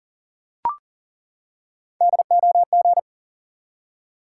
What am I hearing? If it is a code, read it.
Morse code, heard as a keyed tone, about 39 wpm, 700 Hz. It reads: DOG
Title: Three Letter Words - 40wpm